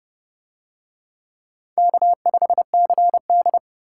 Morse code, heard as a keyed tone, about 29 wpm, 700 Hz. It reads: K5CD